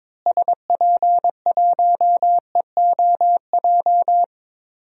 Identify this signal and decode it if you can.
Morse code, heard as a keyed tone, about 22 wpm, 700 Hz. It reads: SP1EOJ
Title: DXCC-100 Call Signs - 6 Characters - Repeated 3x - 22wpm